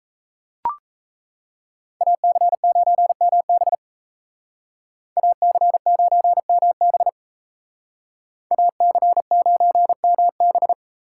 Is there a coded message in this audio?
Yes: Morse code from 2.0 to 10.7 s, heard as a keyed tone, about 35 wpm, 700 Hz. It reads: AC9MB AC9MB AC9MB